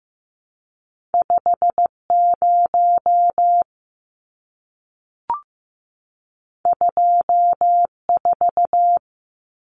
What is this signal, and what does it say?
Morse code, heard as a keyed tone, about 15 wpm, 700 Hz. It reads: 50 24
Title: Numbers - 2-Digit 15wpm